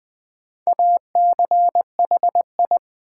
Morse code, heard as a keyed tone, about 20 wpm, 700 Hz. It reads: ACHI
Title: Build A Word - 4 Letters At A Time - 11-Letter Words - 20wpm